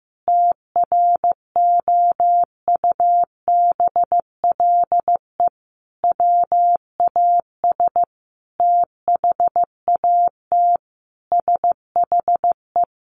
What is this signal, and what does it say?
Morse code, heard as a keyed tone, about 15 wpm, 700 Hz. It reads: TROUBLE WAS THAT SHE
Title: Sentences from Top 1200 Words 15wpm